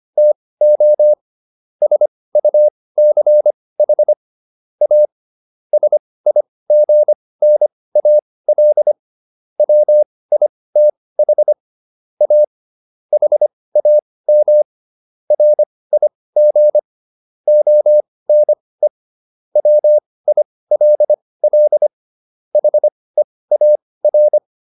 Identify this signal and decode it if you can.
Morse code, heard as a keyed tone, about 25 wpm, spaced as for 18 wpm, 600 Hz. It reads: TO SUCH A SIGNAL WITH A HAM RIG ONE WILL HEAR